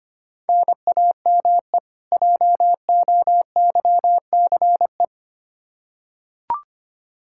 Morse code, heard as a keyed tone, about 25 wpm, 700 Hz. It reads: NAME JOYCE
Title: K1USN SST QSO Practice 25wpm